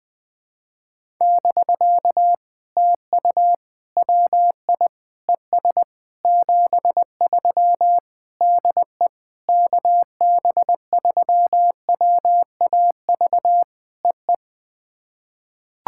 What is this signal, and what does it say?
Morse code, heard as a keyed tone, about 21 wpm, 700 Hz. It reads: <BK> TU WI ES 73 DE KB3WAV EE
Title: POTA QSO Practice - 20wpm